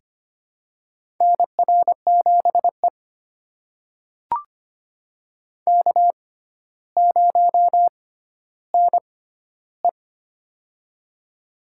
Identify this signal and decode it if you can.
Morse code, heard as a keyed tone, about 25 wpm, spaced as for 13 wpm, 700 Hz. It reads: NR7E K0NE
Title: USA Call Signs - 4 Characters - 6x Character Spacing - 25wpm